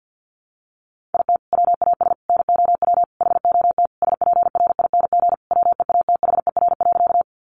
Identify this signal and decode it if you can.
Morse code, heard as a keyed tone, about 48 wpm, 700 Hz. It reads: IT WAS NOW HOT SPRING WEATHER.
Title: Sentences from Top 1200 Words 50wpm